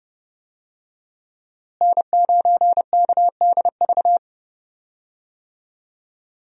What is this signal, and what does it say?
Morse code, heard as a keyed tone, about 30 wpm, 700 Hz. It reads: N9KDV
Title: USA Call Signs - 5 Characters - Repeated 2x - 30wpm